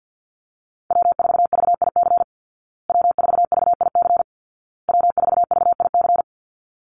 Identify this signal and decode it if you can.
Morse code, heard as a keyed tone, about 50 wpm, 700 Hz. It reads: W4VIC W4VIC W4VIC
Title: Call Signs Repeated 2x - 50wpm